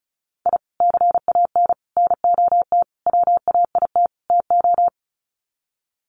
Morse code, heard as a keyed tone, about 34 wpm, 700 Hz. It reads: I CAN NOT WAIT TO
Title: Sets of 5 Words Encore 35wpm